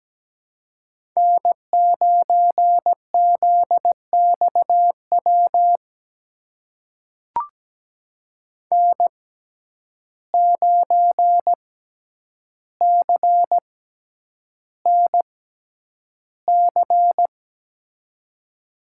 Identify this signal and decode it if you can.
Morse code, heard as a keyed tone, about 17 wpm, spaced as for 9 wpm, 700 Hz. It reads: N9ZXW N9CNC
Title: USA Call Signs - 5 Characters - 6x Character Spacing - 17wpm